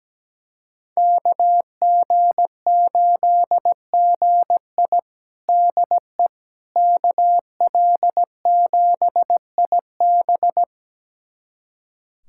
Morse code, heard as a keyed tone, about 17 wpm, 700 Hz. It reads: KG8GI DE KL7IB
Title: USA Callsign DE Callsign 17wpm